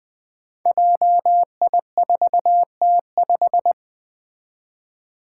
Morse code, heard as a keyed tone, about 20 wpm, 700 Hz. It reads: JI4T5
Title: Five Challenging Letter-Number Combinations 20wpm